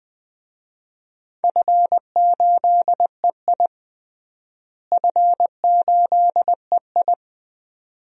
Morse code, heard as a keyed tone, about 20 wpm, 700 Hz. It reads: F8EI F8EI
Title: DXCC-100 Call Signs - 4 Characters - Repeated 2x - 20wpm